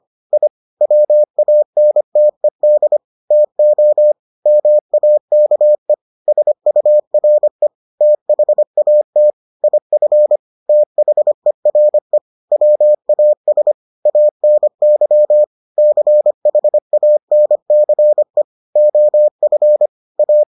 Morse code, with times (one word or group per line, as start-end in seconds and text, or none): 0.3-0.5 I
0.8-3.0 WANTED
3.3-4.1 TO
4.5-5.9 MAKE
6.3-7.7 SURE
8.0-9.3 THAT
9.6-10.4 IF
10.7-12.2 THERE
12.5-13.7 WAS
14.1-15.4 ANY
15.8-18.4 CHANCE
18.8-19.9 OF
20.2-20.4 A